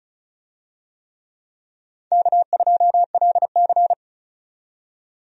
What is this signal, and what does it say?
Morse code, heard as a keyed tone, about 36 wpm, 700 Hz. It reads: K2LC
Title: USA Call Signs - 4 Characters - 35wpm